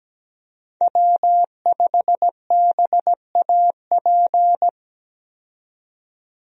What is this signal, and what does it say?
Morse code, heard as a keyed tone, about 17 wpm, 700 Hz. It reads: W5BAP
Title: USA Call Signs - 5 Characters - 17wpm